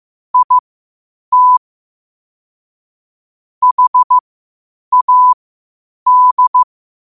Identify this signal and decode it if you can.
Morse code, heard as a keyed tone, about 15 wpm, spaced as for 8 wpm, 1000 Hz. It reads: IT HAD
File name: qotdfiles / 8wpm